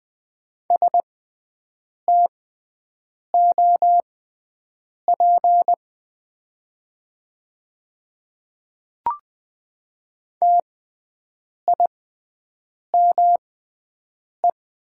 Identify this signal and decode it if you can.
Morse code, heard as a keyed tone, about 20 wpm, spaced as for 7 wpm, 700 Hz. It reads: STOP TIME
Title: 6x Character Spacing - Top 300 Words - 20wpm